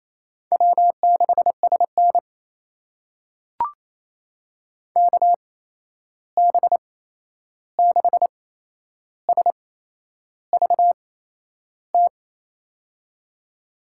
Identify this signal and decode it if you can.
Morse code, heard as a keyed tone, about 28 wpm, spaced as for 11 wpm, 700 Hz. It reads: W6SN KB6SVT
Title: USA Call Signs - 8x Character Spacing - 28wpm